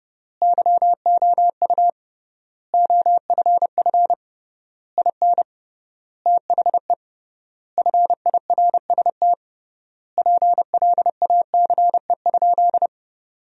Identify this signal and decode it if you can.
Morse code, heard as a keyed tone, about 30 wpm, 700 Hz. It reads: YOU OFF IN THE FIRST PLACE?
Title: 3x Word-Spacing - Sentences from Top 200 Words - 30wpm